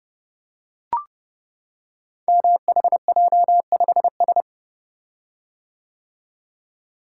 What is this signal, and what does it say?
Morse code, heard as a keyed tone, about 30 wpm, 700 Hz. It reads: MHJ5S
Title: Code Groups - 5 Letter:Numbers - 30wpm